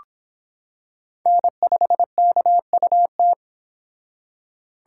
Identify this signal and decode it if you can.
Morse code, heard as a keyed tone, about 23 wpm, 700 Hz. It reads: N5KUT